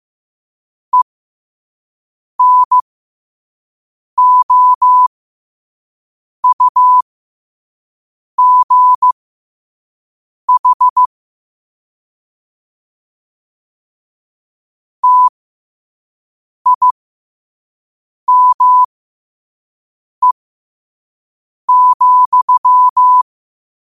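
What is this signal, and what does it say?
Morse code, heard as a keyed tone, about 15 wpm, spaced as for 5 wpm, 1000 Hz. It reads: ENOUGH TIME,
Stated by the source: Morse code sent at 5 wpm for learning.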